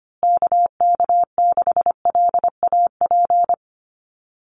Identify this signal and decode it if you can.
Morse code, heard as a keyed tone, about 25 wpm, 700 Hz. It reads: KK6LAP
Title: USA Call Signs Repeated 2x 25wpm